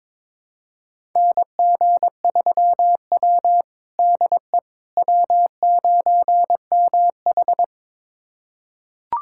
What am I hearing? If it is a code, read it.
Morse code, heard as a keyed tone, about 22 wpm, 700 Hz. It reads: NG3W DE W9MH
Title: USA Callsign DE Callsign - 4 Characters - 22wpm